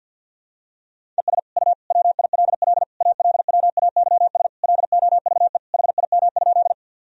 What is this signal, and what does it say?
Morse code, heard as a keyed tone, about 51 wpm, 700 Hz. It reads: ES U WILL ALWAYS LOVE HIM?
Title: Sentences from Top 400 Words and Abbreviations 50wpm